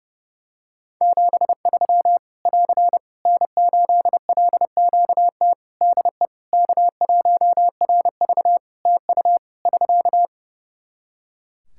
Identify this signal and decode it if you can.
Morse code, heard as a keyed tone, about 30 wpm, 700 Hz. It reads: 73 <AR> N8LQT DE K1RV TU <SK>
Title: K1USN SST QSO Practice 30wpm